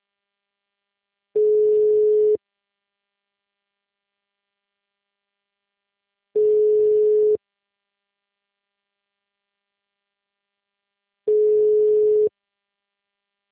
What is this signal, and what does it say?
Ringback tone (Europe, 425 Hz).